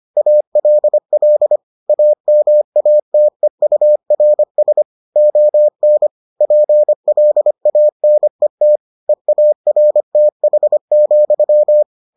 Morse code, heard as a keyed tone, about 25 wpm, 600 Hz. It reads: ALL AMATEURS ON PLANET EARTH,